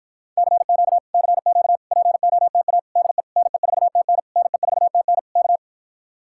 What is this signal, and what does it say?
Morse code, heard as a keyed tone, about 54 wpm, 700 Hz. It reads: CQ CQ POTA DE NE4TN NE4TN K